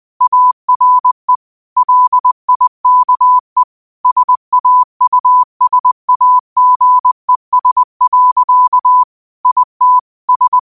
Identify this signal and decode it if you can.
Morse code, heard as a keyed tone, about 20 wpm, 1000 Hz. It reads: ARE LIKE SAUSAGES. IT S